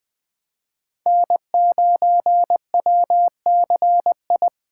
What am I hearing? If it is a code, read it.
Morse code, heard as a keyed tone, about 20 wpm, 700 Hz. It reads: N9WCI
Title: USA Call Signs - 2x Character Spacing - 20wpm